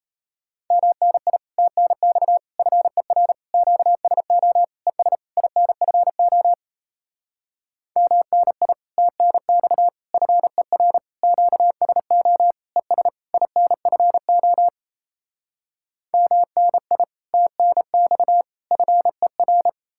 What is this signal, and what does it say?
Morse code, heard as a keyed tone, about 37 wpm, 700 Hz. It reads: MNI TNX FER QSO ES INFO MNI TNX FER QSO ES INFO MNI TNX FER